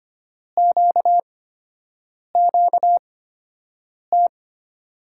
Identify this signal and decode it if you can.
Morse code, heard as a keyed tone, about 25 wpm, spaced as for 7 wpm, 700 Hz. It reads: QQT
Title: USA Call Signs - 5 Characters - 8x Character Spacing - 25wpm